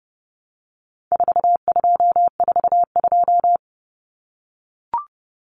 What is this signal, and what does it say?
Morse code, heard as a keyed tone, about 30 wpm, 700 Hz. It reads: 4242
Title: Numbers - 4-Digit 30wpm